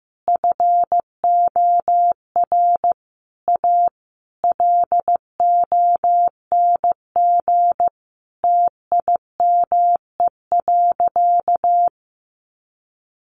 Morse code, heard as a keyed tone, about 15 wpm, 700 Hz. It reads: FOR A LONG TIME.
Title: Sentences from Top 1000 Words 15wpm